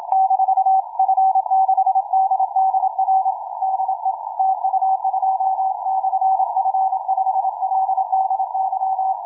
Morse code, heard as a keyed tone, about 26 wpm, 760 Hz. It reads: -F6DDR DE TU2/F5LDY
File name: cw forum / tu2 f5ldy 18mhz 16h46